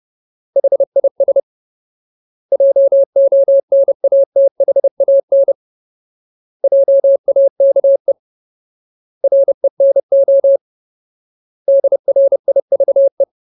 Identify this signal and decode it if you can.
Morse code, heard as a keyed tone, about 30 wpm, 550 Hz. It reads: HIS JONATHAN JAKE RENO DRIVE
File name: Ultimate Word List / 30WPM Ultimate